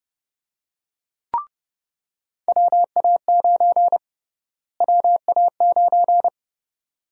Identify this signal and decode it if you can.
Morse code, heard as a keyed tone, about 29 wpm, 700 Hz. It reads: WA9 WA9
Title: USA Call Sign Prefixes - Repeated 2x - 30wpm